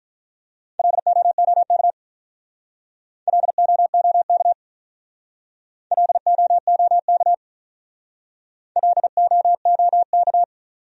Morse code, heard as a keyed tone, about 48 wpm, 700 Hz. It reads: LOOK LOOK LOOK LOOK